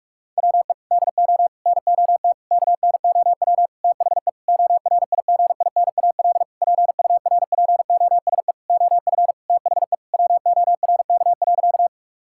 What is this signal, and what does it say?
Morse code, heard as a keyed tone, about 46 wpm, 700 Hz. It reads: WE DO NOT KNOW THE ORIGINAL PURPOSE OF THE WORK.
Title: Sentences from Top 1000 Words - Review Part 6 - 45wpm